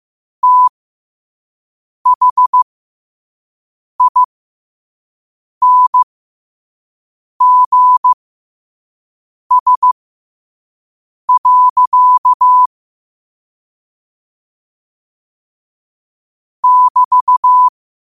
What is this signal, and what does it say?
Morse code, heard as a keyed tone, about 15 wpm, spaced as for 5 wpm, 1000 Hz. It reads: THINGS. <BT>
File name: qotdfiles / 5wpm